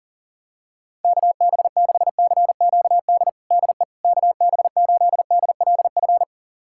Morse code, heard as a keyed tone, about 40 wpm, 700 Hz. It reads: KB6CQD DE KB8DLF
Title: USA Callsign DE Callsign 40wpm